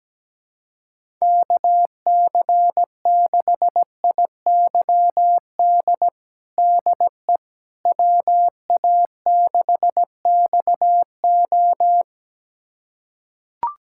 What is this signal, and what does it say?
Morse code, heard as a keyed tone, about 17 wpm, 700 Hz. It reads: KC6IYD DE WA6XO
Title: USA Callsign DE Callsign 17wpm